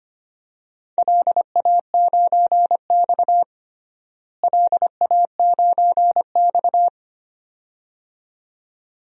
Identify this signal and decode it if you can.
Morse code, heard as a keyed tone, about 25 wpm, 700 Hz. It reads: LA9X LA9X
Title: DXCC-100 Call Signs - 4 Characters - Repeated 2x - 25wpm